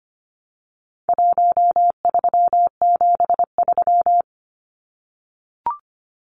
Morse code, heard as a keyed tone, about 25 wpm, 700 Hz. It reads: 1373
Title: Numbers - Contest Style - 4-Digit 25wpm